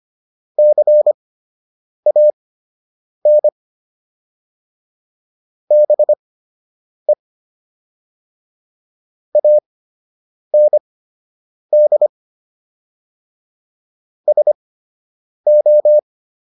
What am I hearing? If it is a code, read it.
Morse code, heard as a keyed tone, about 25 wpm, spaced as for 8 wpm, 600 Hz. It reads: CAN BE AND SO